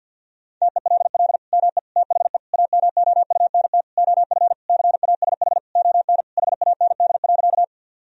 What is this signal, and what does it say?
Morse code, heard as a keyed tone, about 50 wpm, 700 Hz. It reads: TELL ME THE AMOUNT OF CASH ON HAND.
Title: Sentences from Top 1000 Words - Review Part 4 - 50wpm